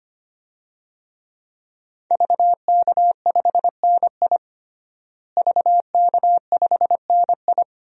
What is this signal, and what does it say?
Morse code, heard as a keyed tone, about 25 wpm, 700 Hz. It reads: VK5NI VK5NI
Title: DXCC-100 Call Signs - 5 Characters - Repeated 2x - 25wpm